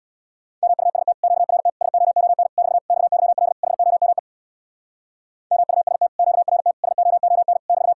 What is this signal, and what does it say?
Morse code, heard as a keyed tone, about 66 wpm, 700 Hz. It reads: MUST QRT SOON <BT> XYL HOME MUST QRT SOON <BT>